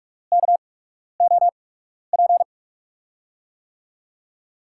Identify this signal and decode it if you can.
Morse code, heard as a keyed tone, about 44 wpm, spaced as for 12 wpm, 690 Hz. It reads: KOP
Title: USA Call Signs - 5 Characters - 8x Character Spacing - 45wpm